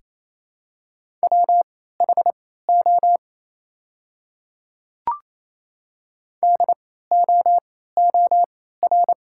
Morse code, heard as a keyed tone, about 28 wpm, spaced as for 16 wpm, 700 Hz. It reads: WHO DOOR
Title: 3x Character Spacing - Top 300 Words - 28wpm